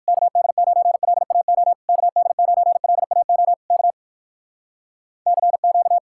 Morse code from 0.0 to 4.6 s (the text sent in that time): KD9LAO KD9LAO K